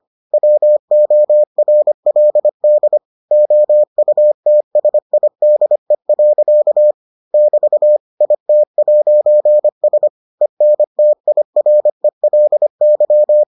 Morse code, heard as a keyed tone, about 25 wpm, 600 Hz. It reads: WORLD OUTSIDE. <BT> IT'S ENTIRELY